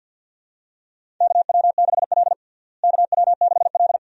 Morse code, heard as a keyed tone, about 52 wpm, 700 Hz. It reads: KW6L KW6L
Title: USA Call Signs - 4 Characters - Repeated 2x - 50wpm